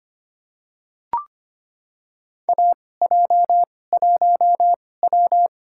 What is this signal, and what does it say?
Morse code, heard as a keyed tone, about 25 wpm, spaced as for 18 wpm, 700 Hz. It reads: AJ1W
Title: USA Call Signs - 4 Characters - 2x Character Spacing - 25wpm